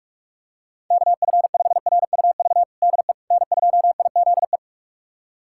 Morse code, heard as a keyed tone, about 42 wpm, 700 Hz. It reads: KF5RUV DE N2IZE
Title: USA Callsign DE Callsign 45wpm